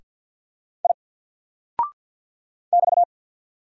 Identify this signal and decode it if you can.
Morse code, heard as a keyed tone, about 50 wpm, 700 Hz. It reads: I <BT>
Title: Single Character (Letter:Number:Punctuation:Prosign) 50wpm